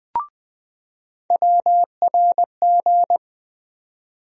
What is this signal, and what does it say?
Morse code, heard as a keyed tone, about 21 wpm, 700 Hz. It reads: WRG
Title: USA Call Sign Postfixes - 3 Letters 20wpm